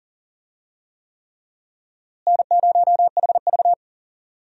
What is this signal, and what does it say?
Morse code, heard as a keyed tone, about 40 wpm, 700 Hz. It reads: N0HV
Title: USA Call Signs - 4 Characters - Repeated 3x - 40wpm